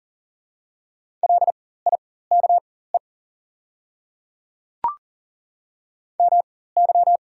Morse code, heard as a keyed tone, about 40 wpm, spaced as for 19 wpm, 690 Hz. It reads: LIKE MY
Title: 4x Character Spacing - Top 100 Words - 40wpm